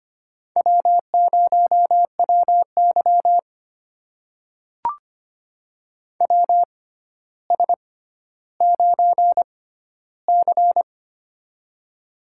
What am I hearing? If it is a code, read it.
Morse code, heard as a keyed tone, about 25 wpm, spaced as for 13 wpm, 700 Hz. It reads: W0WY WS9C
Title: USA Call Signs - 4 Characters - 6x Character Spacing - 25wpm